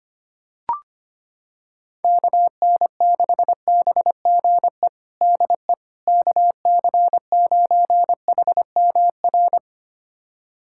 Morse code, heard as a keyed tone, about 25 wpm, 700 Hz. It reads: KN6BGE DE KC9HMR
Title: USA Callsign DE Callsign 25wpm